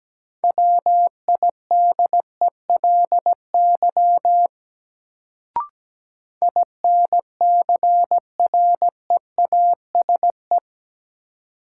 Morse code, heard as a keyed tone, about 17 wpm, 700 Hz. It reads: WIDELY INCREASE
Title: Sentences from Top 1200 Words - New Word Drill Alt - 17wpm